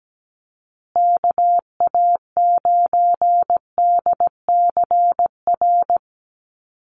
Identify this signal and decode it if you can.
Morse code, heard as a keyed tone, about 17 wpm, 700 Hz. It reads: KA9DCR
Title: USA Call Signs Repeated 2x 17wpm